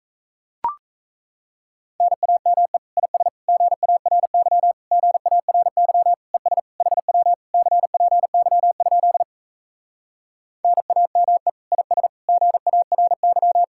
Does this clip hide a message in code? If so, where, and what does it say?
Yes: Morse code from 2.0 to 13.6 s, heard as a keyed tone, about 37 wpm, 700 Hz. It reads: NAME IS GARY GARY ES HW CPY? NAME IS GARY